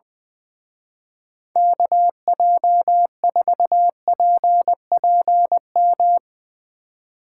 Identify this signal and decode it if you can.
Morse code, heard as a keyed tone, about 21 wpm, 700 Hz. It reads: KJ4PPM
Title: Speed Racing - USA Call Signs 20wpm